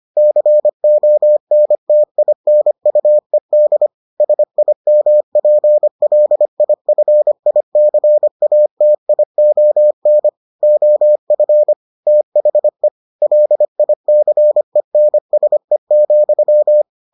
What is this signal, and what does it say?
Morse code, heard as a keyed tone, about 25 wpm, 600 Hz. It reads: CONTINUED SIMPLIFICATION OF THE LICENSE,